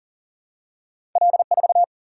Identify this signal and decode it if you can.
Morse code, heard as a keyed tone, about 40 wpm, 700 Hz. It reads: L4
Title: Single-Letter Single-Number 40wpm